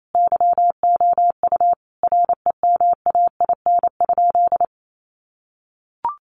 Morse code, heard as a keyed tone, about 28 wpm, 700 Hz. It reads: YOU REMAIN?
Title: Sentences from Top 1200 Words 28wpm